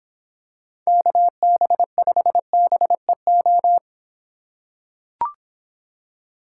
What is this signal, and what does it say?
Morse code, heard as a keyed tone, about 25 wpm, 700 Hz. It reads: KB5BEO